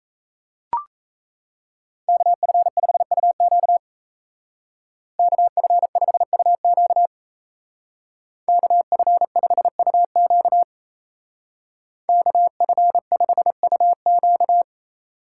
Morse code, heard as a keyed tone, about 34 wpm, 700 Hz. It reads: KF5UQ KF5UQ KF5UQ KF5UQ
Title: Speed Racing - USA 2x2 Call Signs 28wpm